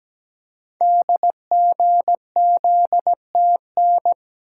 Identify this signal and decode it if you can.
Morse code, heard as a keyed tone, about 17 wpm, 700 Hz. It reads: DGZTN
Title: Code Groups - 5 Letter:Numbers - 17wpm